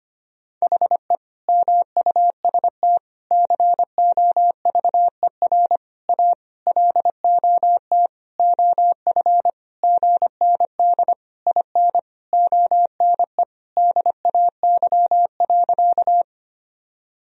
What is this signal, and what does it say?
Morse code, heard as a keyed tone, about 25 wpm, 700 Hz. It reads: HE MUST COVER A LOT OF GND IN ONE DAY.
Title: Sentences from Top 500 Words and Abbreviations 25wpm